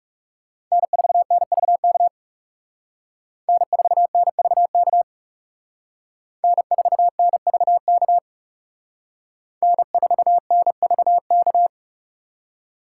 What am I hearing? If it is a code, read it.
Morse code, heard as a keyed tone, about 39 wpm, 700 Hz. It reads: N4NVK N4NVK N4NVK N4NVK